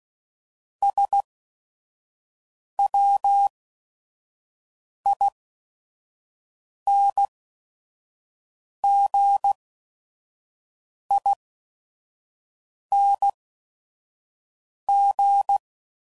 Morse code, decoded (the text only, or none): SWINGING